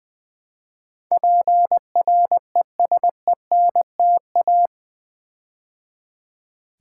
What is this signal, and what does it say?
Morse code, heard as a keyed tone, about 20 wpm, 700 Hz. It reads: PRESENTA
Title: Build A Word - 4 Letters At A Time - 12-Letter Words - 20wpm